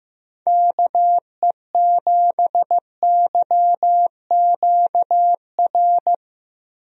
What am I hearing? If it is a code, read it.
Morse code, heard as a keyed tone, about 16 wpm, 700 Hz. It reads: KE7YQR
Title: USA Call Signs - 6 Characters - Repeated 2x - 15wpm